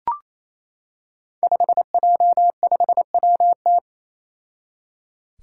Morse code, heard as a keyed tone, about 28 wpm, 700 Hz. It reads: 5J5WT
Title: Five Challenging Letter-Number Combinations 28wpm